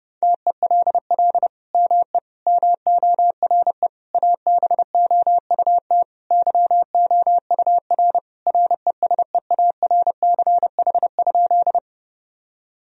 Morse code, heard as a keyed tone, about 32 wpm, 700 Hz. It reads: TELL ME MORE ABOUT YOUR RESEARCH?
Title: Sentences from Top 1000 Words - Review Part 4 - 30wpm